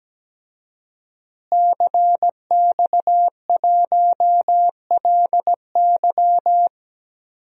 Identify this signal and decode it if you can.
Morse code, heard as a keyed tone, about 17 wpm, 700 Hz. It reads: CX1LY
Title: DXCC-100 Call Signs - 5 Characters - Repeated 2x - 17wpm